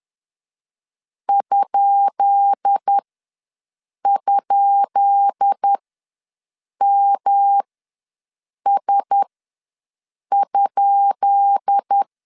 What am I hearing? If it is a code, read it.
Morse code, heard as a keyed tone, about 10 wpm, spaced as for 6 wpm, 800 Hz. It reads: ??MS?
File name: LEZIONI / 10 SIGNS / 10B INT ws 10wpm 4134